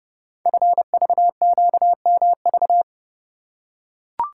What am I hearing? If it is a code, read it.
Morse code, heard as a keyed tone, about 28 wpm, 700 Hz. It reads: FVQMV
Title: Code Groups - 5 Letter:Numbers - 30wpm